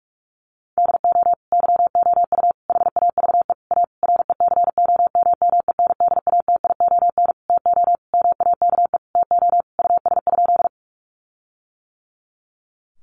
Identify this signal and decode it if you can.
Morse code, heard as a keyed tone, about 44 wpm, 700 Hz. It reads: DO YOU HAVE A RECOMMENDATION TO MAKE TO US?
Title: Sentences from Top 1400 Words 45wpm